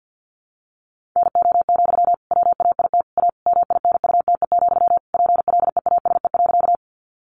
Morse code, heard as a keyed tone, about 50 wpm, 700 Hz. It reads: NO, WAIT A MINUTE, PLEASE.
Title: Sentences from Top 1500 Words 50wpm